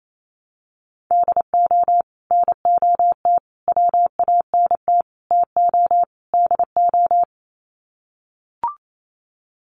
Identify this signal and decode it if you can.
Morse code, heard as a keyed tone, about 28 wpm, 700 Hz. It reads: DO NOT WANT TO DO
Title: Sets of 5 Words 28wpm